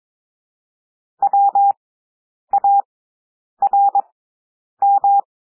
Morse code, heard as a keyed tone, about 22 wpm, spaced as for 9 wpm, 800 Hz. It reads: WARM
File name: cwmp3 / W2 SOUND WORDS 22 wpm